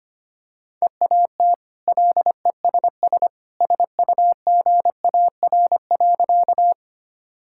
Morse code, heard as a keyed tone, about 25 wpm, 690 Hz. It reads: EAT LESS SUGAR.
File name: Easy Sentences with Hints - 25wpm